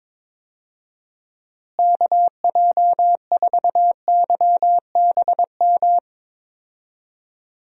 Morse code, heard as a keyed tone, about 22 wpm, 700 Hz. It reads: KJ4YBM
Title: USA Call Signs - 6 Characters - Repeated 2x - 22wpm